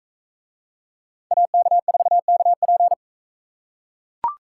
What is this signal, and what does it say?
Morse code, heard as a keyed tone, about 42 wpm, 700 Hz. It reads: AK4KP